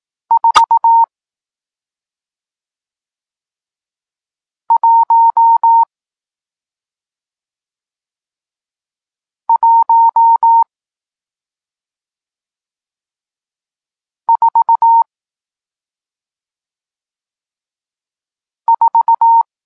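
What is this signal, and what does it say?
Morse code, heard as a keyed tone, about 18 wpm, spaced as for 2 wpm, 920 Hz. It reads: 41144
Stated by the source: Morse code lesson teaching digits 1, 4 and 6, from SP9MOA, Niepołomicki Klub Krótkofalowców (Poland) amateur radio club.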